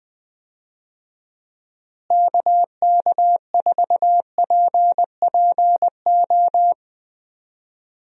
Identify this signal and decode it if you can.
Morse code, heard as a keyed tone, about 20 wpm, 700 Hz. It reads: KK4PPO
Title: USA Call Signs - 6 Characters - Repeated 2x - 20wpm